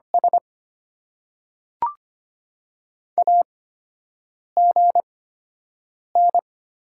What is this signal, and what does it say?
Morse code, heard as a keyed tone, about 25 wpm, spaced as for 7 wpm, 690 Hz. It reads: S AGN
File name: Top 90 QSO Elements - 8x Character Spacing - 25wpm